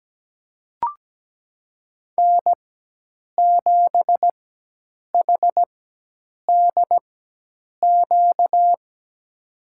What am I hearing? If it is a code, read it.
Morse code, heard as a keyed tone, about 17 wpm, spaced as for 8 wpm, 700 Hz. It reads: N7HDQ
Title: USA Call Signs - 5 Characters - 4x Character Spacing - 17wpm